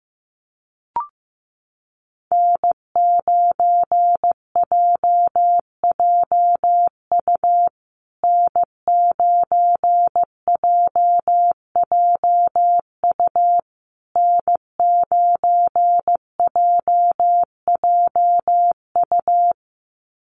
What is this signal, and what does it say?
Morse code, heard as a keyed tone, about 15 wpm, 700 Hz. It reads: N9JJU N9JJU N9JJU
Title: USA Call Signs Repeated 3x 15wpm